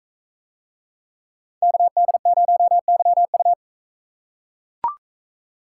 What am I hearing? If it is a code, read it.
Morse code, heard as a keyed tone, about 39 wpm, 700 Hz. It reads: KD0YU